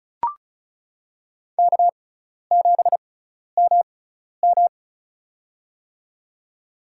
Morse code, heard as a keyed tone, about 36 wpm, spaced as for 12 wpm, 700 Hz. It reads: K7MM
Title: USA Call Signs - 4 Characters - 6x Character Spacing - 35wpm